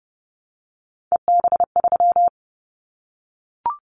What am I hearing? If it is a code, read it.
Morse code, heard as a keyed tone, about 30 wpm, 700 Hz. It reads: EB3
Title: Callsign Prefixes for Top 25 DXCC - Repeated 3x - 30wpm